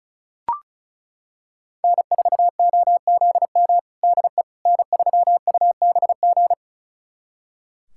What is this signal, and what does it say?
Morse code, heard as a keyed tone, about 36 wpm, 700 Hz. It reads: N4OZM DE N3UBG
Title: USA Callsign DE Callsign - 5 Characters - 35wpm